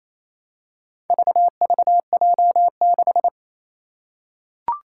Morse code, heard as a keyed tone, about 28 wpm, 700 Hz. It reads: VVJ6
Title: Four Challenging Letter-Number Combinations 28wpm